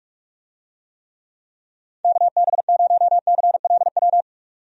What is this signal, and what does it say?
Morse code, heard as a keyed tone, about 46 wpm, 700 Hz. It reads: KB0CLW
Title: USA Call Signs - 6 Characters - 45wpm